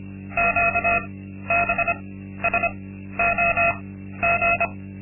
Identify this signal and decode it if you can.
Morse code, heard as a keyed tone, about 25 wpm, spaced as for 13 wpm, 690 Hz. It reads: QBSOG
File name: Linksmakalnis-2